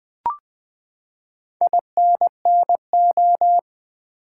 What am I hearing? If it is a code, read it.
Morse code, heard as a keyed tone, about 20 wpm, 700 Hz. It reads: INNO
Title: Build A Word - 4 Letters At A Time - 10-Letter Words - 20wpm